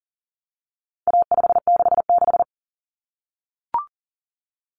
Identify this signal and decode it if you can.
Morse code, heard as a keyed tone, about 40 wpm, 700 Hz. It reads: A566
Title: Common Cut Numbers - 4-Digit 40wpm